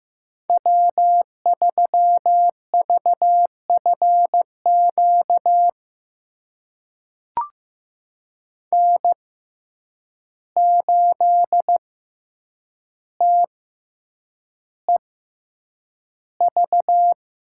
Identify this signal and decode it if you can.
Morse code, heard as a keyed tone, about 15 wpm, spaced as for 8 wpm, 700 Hz. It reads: W3VFQ N8TEV
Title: USA Call Signs - 5 Characters - 6x Character Spacing - 15wpm